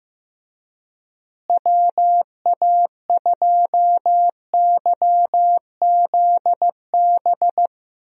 Morse code, heard as a keyed tone, about 16 wpm, 700 Hz. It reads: WA2YZB
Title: USA Call Signs - 6 Characters - Repeated 3x - 15wpm